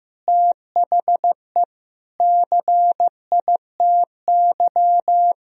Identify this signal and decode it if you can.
Morse code, heard as a keyed tone, about 15 wpm, 700 Hz. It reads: THE CITY
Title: Sentences from Top 1000 Words - Review Part 4 - 15wpm